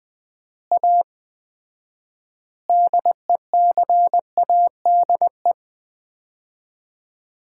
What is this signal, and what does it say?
Morse code, heard as a keyed tone, about 19 wpm, 700 Hz. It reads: A DECADE
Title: 4x Word-Spacing - Sets of 5 Words - 20wpm